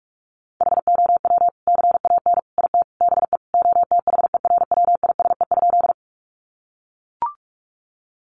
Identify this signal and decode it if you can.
Morse code, heard as a keyed tone, about 44 wpm, 700 Hz. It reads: HOW CAN IT BE OTHERWISE?
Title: Sentences from Top 1200 Words 45wpm